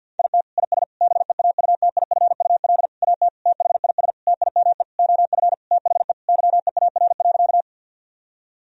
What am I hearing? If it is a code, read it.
Morse code, heard as a keyed tone, about 51 wpm, 700 Hz. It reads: IT IS BEAUTIFUL AT THIS TIME OF THE YEAR.
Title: Sentences from Top 1000 Words - Review Part 1 - 50wpm